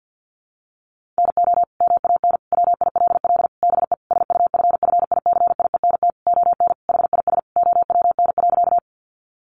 Morse code, heard as a keyed tone, about 50 wpm, 700 Hz. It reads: NO MAN WILL BE SUFFICIENT ON HIS OWN.
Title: Sentences from Top 1400 Words 50wpm